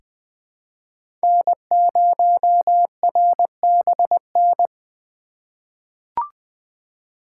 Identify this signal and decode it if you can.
Morse code, heard as a keyed tone, about 20 wpm, 700 Hz. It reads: N0RBN
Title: USA Call Signs - 5 Characters - 2x Character Spacing - 20wpm